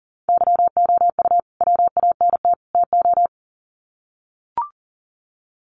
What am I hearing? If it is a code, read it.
Morse code, heard as a keyed tone, about 38 wpm, 700 Hz. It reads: YOU WANT TO
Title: Sets of 3 Words 40wpm